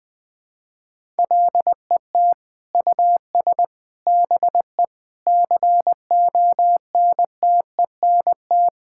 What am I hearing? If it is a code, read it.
Morse code, heard as a keyed tone, about 21 wpm, 700 Hz. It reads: LET US BE CONTENT
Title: Sentences from Top 1000 Words - Review Part 6 - 20wpm